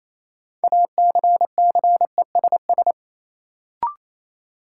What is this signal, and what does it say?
Morse code, heard as a keyed tone, about 28 wpm, 700 Hz. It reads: ACCESS
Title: Sentences from Top 1000 Words - Review Part 4 - New Word Drill - 28wpm